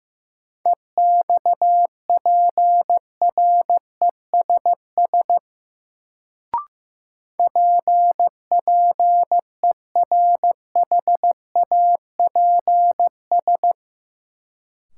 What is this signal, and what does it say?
Morse code, heard as a keyed tone, about 15 wpm, 700 Hz. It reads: EXPRESS PPERHAPS
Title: Sentences from Top 1100 Words - New Word Drill - 15wpm